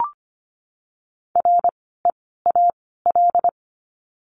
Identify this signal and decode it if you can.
Morse code, heard as a keyed tone, about 25 wpm, spaced as for 16 wpm, 700 Hz. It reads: REAL
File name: course / 25:10wpm / 094 Lesson 16f - Letter c - Words 25wpm